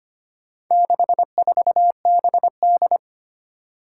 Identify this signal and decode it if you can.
Morse code, heard as a keyed tone, about 25 wpm, 700 Hz. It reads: 64BD
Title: Four Challenging Letter-Number Combinations 25wpm